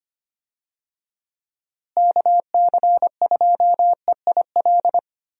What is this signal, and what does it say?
Morse code, heard as a keyed tone, about 25 wpm, 700 Hz. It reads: KC2EIL
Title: USA Call Signs - 6 Characters - Repeated 2x - 25wpm